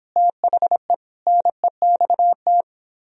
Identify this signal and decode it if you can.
Morse code, heard as a keyed tone, about 31 wpm, 700 Hz. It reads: THE NEXT